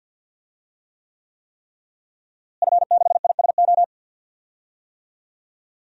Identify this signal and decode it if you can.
Morse code, heard as a keyed tone, about 48 wpm, 700 Hz. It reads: F6ISO
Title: DXCC-100 Call Signs - 5 Characters - Repeated 3x - 50wpm